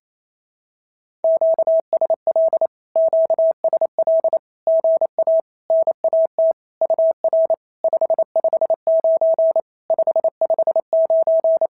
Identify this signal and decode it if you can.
Morse code, heard as a keyed tone, about 27 wpm, 650 Hz. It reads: QSL QSL GA NAT UR 559 559
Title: SOTA QSO Practice 28wpm